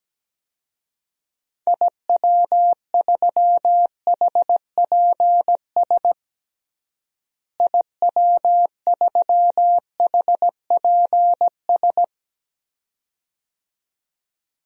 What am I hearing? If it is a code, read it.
Morse code, heard as a keyed tone, about 17 wpm, 700 Hz. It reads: IW3HPS IW3HPS
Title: DXCC-100 Call Signs - 6 Characters - Repeated 2x - 17wpm